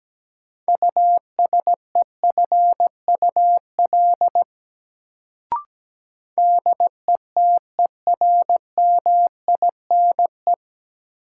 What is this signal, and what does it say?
Morse code, heard as a keyed tone, about 17 wpm, 700 Hz. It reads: USEFUL DETERMINE
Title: Sentences from Top 700 Words - New Word Drill - 17wpm